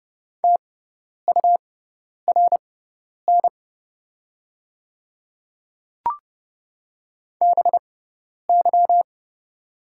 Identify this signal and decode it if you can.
Morse code, heard as a keyed tone, about 30 wpm, spaced as for 10 wpm, 700 Hz. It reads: TURN BY
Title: 6x Character Spacing - Top 200 Words - 30wpm